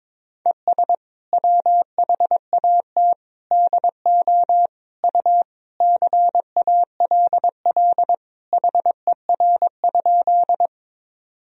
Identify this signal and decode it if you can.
Morse code, heard as a keyed tone, about 22 wpm, 700 Hz. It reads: ES WHAT DO U CALL HER?
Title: Sentences from Top 100 Words and Abbreviations 22wpm